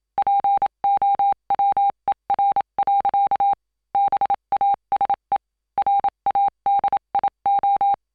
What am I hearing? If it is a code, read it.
Morse code, heard as a keyed tone, about 27 wpm, 800 Hz. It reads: POWER. BASE RADIO